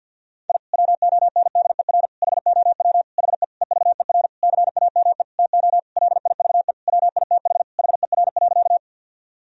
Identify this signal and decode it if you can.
Morse code, heard as a keyed tone, about 50 wpm, 700 Hz. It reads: I WONDER HOW HE EVER CAME TO LIVE WITH HER.
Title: Sentences from Top 1000 Words - Review Part 3 - 50wpm